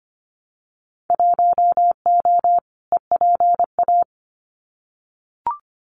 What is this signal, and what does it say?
Morse code, heard as a keyed tone, about 25 wpm, 700 Hz. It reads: 1O EPA
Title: Winter Field Day Exchanges 25wpm